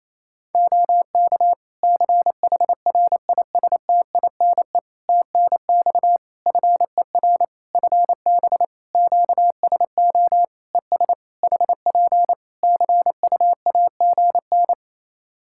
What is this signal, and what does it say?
Morse code, heard as a keyed tone, about 28 wpm, 700 Hz. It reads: OK CHRISTINE TNX FER FB QSO ES HP CUAGN
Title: Practice QSO - Exchange 4 - Ending - 28wpm